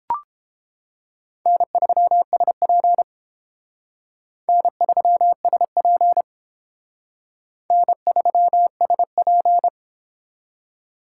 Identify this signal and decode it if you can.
Morse code, heard as a keyed tone, about 29 wpm, 700 Hz. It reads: N3SP N3SP N3SP